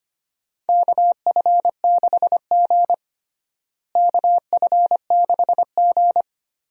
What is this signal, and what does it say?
Morse code, heard as a keyed tone, about 25 wpm, 700 Hz. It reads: KF6G KF6G
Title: USA Call Signs - 4 Characters - Repeated 3x - 25wpm